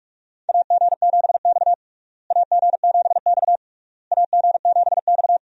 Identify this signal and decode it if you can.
Morse code, heard as a keyed tone, about 41 wpm, 700 Hz. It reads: AG7X AG7X AG7X
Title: USA Call Signs - 4 Characters - Repeated 3x - 45wpm